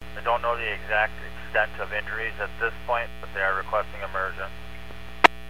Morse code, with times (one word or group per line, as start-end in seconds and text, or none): none